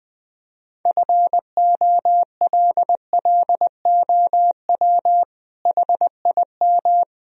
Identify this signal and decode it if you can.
Morse code, heard as a keyed tone, about 21 wpm, 700 Hz. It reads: FOLLOW HIM
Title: Sentences from Top 1000 Words - Review Part 3 - 20wpm